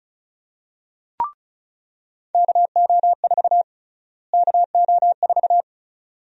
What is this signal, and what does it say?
Morse code, heard as a keyed tone, about 36 wpm, 700 Hz. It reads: KO4 KO4
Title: USA Call Sign Prefixes - Repeated 2x - 35wpm